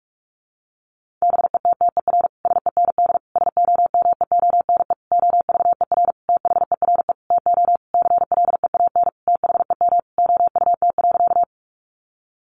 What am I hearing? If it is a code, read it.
Morse code, heard as a keyed tone, about 44 wpm, 700 Hz. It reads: BETTER SEND SOMEONE OVER THERE TO CLEAN THEM OUT.
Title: Sentences from Top 1500 Words 45wpm